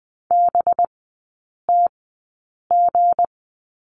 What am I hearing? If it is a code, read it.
Morse code, heard as a keyed tone, about 20 wpm, spaced as for 8 wpm, 700 Hz. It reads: BTG
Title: 191 Lesson 31e - Letter v - Three Characters 20wpm